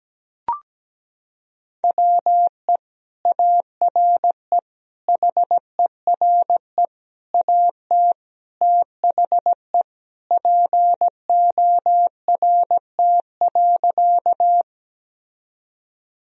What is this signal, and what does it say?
Morse code, heard as a keyed tone, about 17 wpm, 700 Hz. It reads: WE ARE HERE AT THE PORT.
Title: Sentences from Top 1000 Words - Review Part 1 - 17wpm